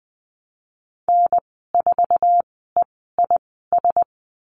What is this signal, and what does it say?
Morse code, heard as a keyed tone, about 20 wpm, spaced as for 14 wpm, 700 Hz. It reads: N4EIS
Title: 069 Lesson 12g - Letter d - Call Signs 20wpm